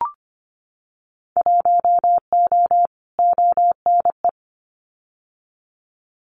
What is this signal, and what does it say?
Morse code, heard as a keyed tone, about 25 wpm, 700 Hz. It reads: 1O ONE
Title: Winter Field Day Exchanges 25wpm